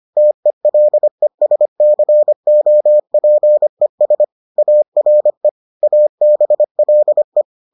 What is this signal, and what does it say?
Morse code, heard as a keyed tone, about 25 wpm, 600 Hz. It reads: TELESCOPES ARE ABLE